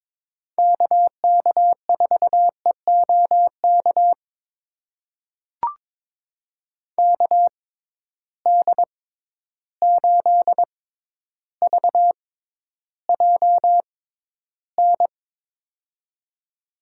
Morse code, heard as a keyed tone, about 21 wpm, spaced as for 11 wpm, 700 Hz. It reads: KK4EOK KD8VJN
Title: USA Call Signs - 6 Characters - 6x Character Spacing - 22wpm